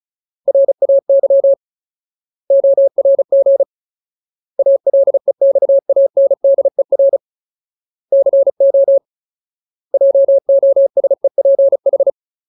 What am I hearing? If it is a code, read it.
Morse code, heard as a keyed tone, about 34 wpm, 550 Hz. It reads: RAY ORG ALEXANDER CO JOSEPH
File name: Ultimate Word List / 35WPM Ultimate